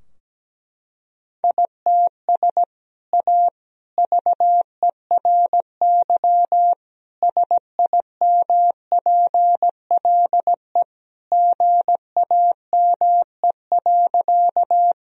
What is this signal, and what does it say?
Morse code, heard as a keyed tone, about 17 wpm, 700 Hz. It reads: ITS A VERY SIMPLE GAME.
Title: Sentences from Top 1000 Words - Review Part 3 - 17wpm